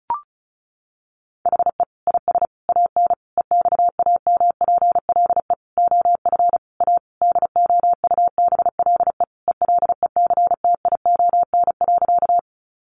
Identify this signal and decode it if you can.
Morse code, heard as a keyed tone, about 34 wpm, 700 Hz. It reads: HE IS AN EXAMPLE OF A DOUBLE ELECTION.
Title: Sentences from Top 1500 Words 35wpm